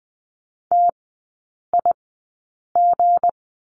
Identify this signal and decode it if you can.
Morse code, heard as a keyed tone, about 20 wpm, spaced as for 8 wpm, 700 Hz. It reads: TIG
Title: 172 Lesson 28e - Number 9 - Three Characters 20wpm